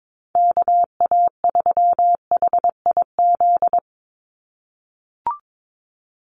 Morse code, heard as a keyed tone, about 22 wpm, 700 Hz. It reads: KA3HIZ
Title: USA Call Signs 22wpm